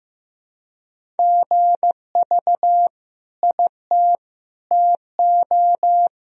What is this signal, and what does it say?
Morse code, heard as a keyed tone, about 15 wpm, 700 Hz. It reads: GV IT TO
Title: Sentences from Top 200 Words and Abbreviations 15wpm